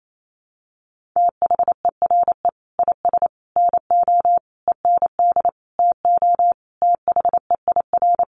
Morse code, heard as a keyed tone, about 28 wpm, 700 Hz. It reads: THERE IS NO END TO THEIR
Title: Sentences from Top 1100 Words 28wpm